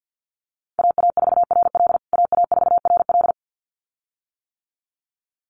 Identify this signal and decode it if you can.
Morse code, heard as a keyed tone, about 50 wpm, 700 Hz. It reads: AA4RL AA4RL
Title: USA Call Signs Repeated 2x 50wpm